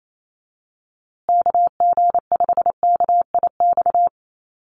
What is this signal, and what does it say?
Morse code, heard as a keyed tone, about 28 wpm, 700 Hz. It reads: KG5KIX
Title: USA Call Signs Repeated 2x 28wpm